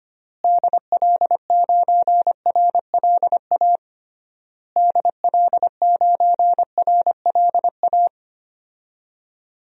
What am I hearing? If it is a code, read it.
Morse code, heard as a keyed tone, about 25 wpm, 700 Hz. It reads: DL9RLA DL9RLA
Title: DXCC-100 Call Signs - 6 Characters - Repeated 3x - 25wpm